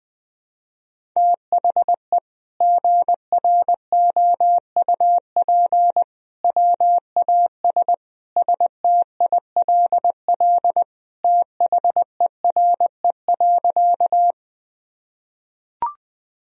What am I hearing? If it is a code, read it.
Morse code, heard as a keyed tone, about 20 wpm, 700 Hz. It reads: THE GROUP WAS STILL THERE.
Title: Sentences from Top 1000 Words - Review Part 1 - 20wpm